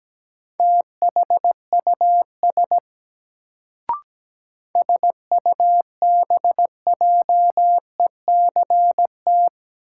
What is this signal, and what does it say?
Morse code, heard as a keyed tone, about 17 wpm, 700 Hz. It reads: THUS SUBJECT
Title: Sentences from Top 800 Words - New Word Drill - 17wpm